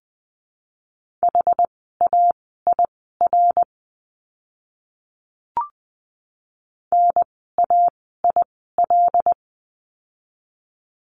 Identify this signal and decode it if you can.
Morse code, heard as a keyed tone, about 20 wpm, spaced as for 14 wpm, 700 Hz. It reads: HAIR NAIL
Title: 075 Lesson 13f - Letter L - Words 20wpm